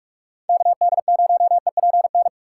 Morse code, heard as a keyed tone, about 46 wpm, 700 Hz. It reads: KD0EPN
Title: USA Call Signs - 8x Character Spacing - 45wpm